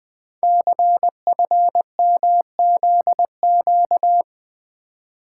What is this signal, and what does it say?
Morse code, heard as a keyed tone, about 21 wpm, 700 Hz. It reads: CFMZQ
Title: Code Groups - 5 Letters - 20wpm